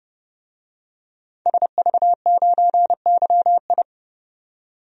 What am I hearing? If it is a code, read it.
Morse code, heard as a keyed tone, about 30 wpm, 700 Hz. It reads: SV9YI
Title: DXCC-100 Call Signs - 5 Characters - 30wpm